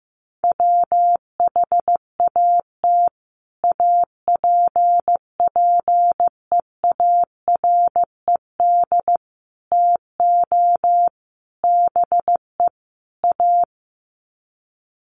Morse code, heard as a keyed tone, about 15 wpm, 700 Hz. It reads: WHAT APPEARED TO BE A
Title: Sets of 5 Words Encore2 15wpm